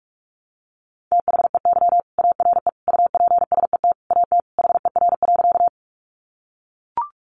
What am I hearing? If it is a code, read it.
Morse code, heard as a keyed tone, about 44 wpm, 700 Hz. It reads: THEY ARE UPSET AT HER.
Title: Sentences from Top 800 Words 45wpm